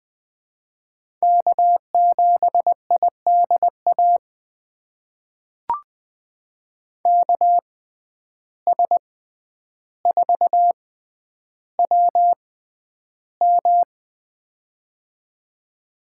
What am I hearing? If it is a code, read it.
Morse code, heard as a keyed tone, about 21 wpm, spaced as for 10 wpm, 700 Hz. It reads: K7IDA KS4WM
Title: USA Call Signs - 5 Characters - 6x Character Spacing - 20wpm